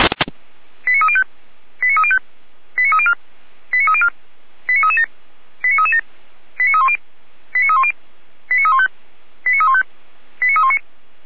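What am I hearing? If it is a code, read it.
Five-tone selective call (ZVEI), heard as quick runs of five tones, tones of about 70 ms. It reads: 89305 89305 89304 89304 89337 89337 89310 89310 89315 89315 89319
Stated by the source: ZVEI five-tone call 89246 received by SDR.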